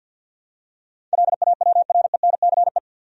Morse code, heard as a keyed tone, about 53 wpm, 700 Hz. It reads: LAWRENCE
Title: USA Common Names - 50wpm